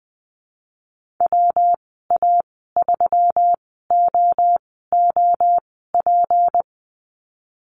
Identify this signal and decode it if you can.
Morse code, heard as a keyed tone, about 20 wpm, spaced as for 15 wpm, 700 Hz. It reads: WA3OOP
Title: 167 Lesson 27g - Number 7 - Call Signs 20wpm